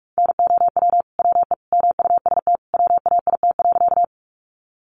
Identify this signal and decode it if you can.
Morse code, heard as a keyed tone, about 44 wpm, 700 Hz. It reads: NOW WE MUST WAIT.
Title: Sentences from Top 500 Words 45wpm